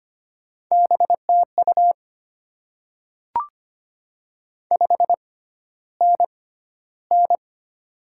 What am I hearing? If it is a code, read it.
Morse code, heard as a keyed tone, about 25 wpm, spaced as for 13 wpm, 700 Hz. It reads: BTU 5NN
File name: Top 20 QSO Elements - 6x Character Spacing - 25wpm